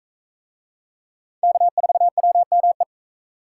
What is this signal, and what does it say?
Morse code, heard as a keyed tone, about 42 wpm, 700 Hz. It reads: K4WME